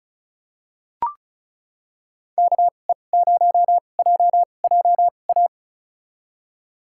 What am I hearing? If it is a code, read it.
Morse code, heard as a keyed tone, about 36 wpm, spaced as for 25 wpm, 700 Hz. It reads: KE0JJA
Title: USA Call Signs - 6 Characters - 2x Character Spacing - 35wpm